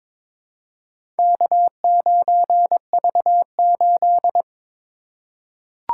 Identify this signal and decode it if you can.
Morse code, heard as a keyed tone, about 22 wpm, 700 Hz. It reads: K9V8
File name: practice / 22wpm / Four Letter-Number Combinations - 22wpm